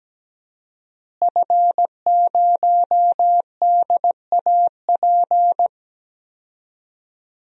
Morse code, heard as a keyed tone, about 17 wpm, 700 Hz. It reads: F0DAP
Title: DXCC-100 Call Signs - 5 Characters - Repeated 2x - 17wpm